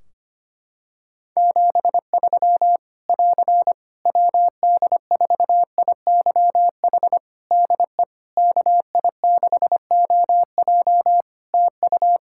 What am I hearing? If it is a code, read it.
Morse code, heard as a keyed tone, about 25 wpm, 700 Hz. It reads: 73 <AR> WD4IYH DE KI6OJ TU
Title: K1USN SST QSO Practice 25wpm